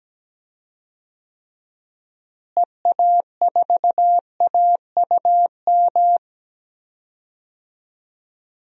Morse code, heard as a keyed tone, about 17 wpm, 700 Hz. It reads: EA4AUM
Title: DXCC-100 Call Signs - 6 Characters - Repeated 3x - 17wpm